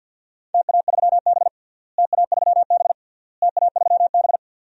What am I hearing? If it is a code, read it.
Morse code, heard as a keyed tone, about 52 wpm, 700 Hz. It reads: TA3B TA3B TA3B
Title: DXCC-100 Call Signs - 4 Characters - Repeated 3x - 50wpm